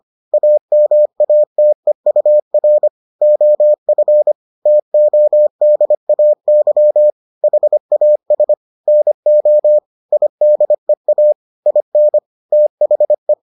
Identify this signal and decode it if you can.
Morse code, heard as a keyed tone, about 25 wpm, 600 Hz. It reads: AMATEUR OF TODAY HAS NO IDEA IN THE